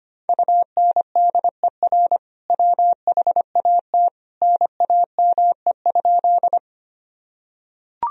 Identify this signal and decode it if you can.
Morse code, heard as a keyed tone, about 25 wpm, 700 Hz. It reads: UNDER WHAT NAME?
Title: Sentences from Top 1000 Words - Review Part 1 - 25wpm